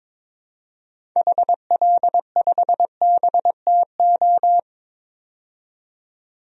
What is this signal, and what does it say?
Morse code, heard as a keyed tone, about 22 wpm, 700 Hz. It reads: HL5BTO
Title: DXCC-100 Call Signs - 6 Characters - 22wpm